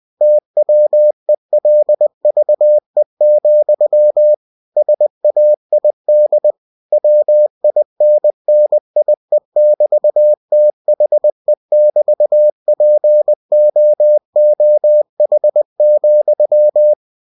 Morse code, heard as a keyed tone, about 20 wpm, 600 Hz. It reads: TWELVE, SAID WINNIE<BT>THE<BT>POOH,
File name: cw-books / winnie-the-pooh / winnie-20wpm-page-0024